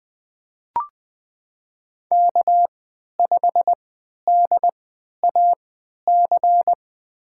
Morse code, heard as a keyed tone, about 20 wpm, spaced as for 11 wpm, 700 Hz. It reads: K5DAC
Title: USA Call Signs - 5 Characters - 3x Character Spacing - 20wpm